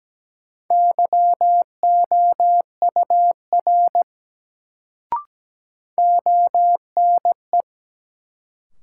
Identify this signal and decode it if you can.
Morse code, heard as a keyed tone, about 17 wpm, 700 Hz. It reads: YOUR ONE
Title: Sets of 2 Words - New Word Drill - 17wpm